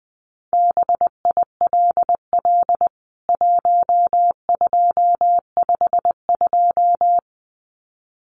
Morse code, heard as a keyed tone, about 20 wpm, 700 Hz. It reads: BILL 1252
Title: CWT Exchange 20wpm